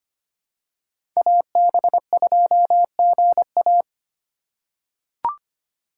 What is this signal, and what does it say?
Morse code, heard as a keyed tone, about 25 wpm, 700 Hz. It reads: AB2GA
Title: USA Call Signs - 5 Characters - 4x Character Spacing - 25wpm